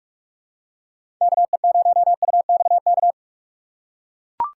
Morse code, heard as a keyed tone, about 41 wpm, 700 Hz. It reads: KE0UXK
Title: USA Call Signs - 6 Characters - 3x Character Spacing - 45wpm